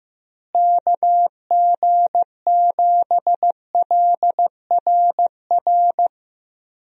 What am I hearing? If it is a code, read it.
Morse code, heard as a keyed tone, about 15 wpm, 700 Hz. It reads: KG7LRR
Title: USA Call Signs - 6 Characters - Repeated 3x - 15wpm